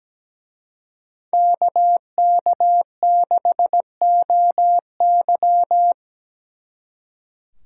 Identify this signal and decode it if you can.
Morse code, heard as a keyed tone, about 20 wpm, 700 Hz. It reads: KK6OY
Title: Speed Racing - USA 2x2 Call Signs 17wpm